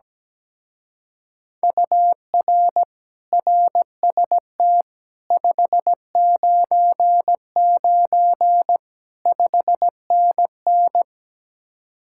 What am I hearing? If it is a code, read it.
Morse code, heard as a keyed tone, about 17 wpm, 700 Hz. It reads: UR RST 599 5NN
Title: Speed-Racing Common QSO Phrases Encore - 17wpm